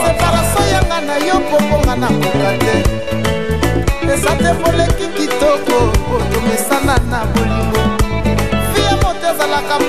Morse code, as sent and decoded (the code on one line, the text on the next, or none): none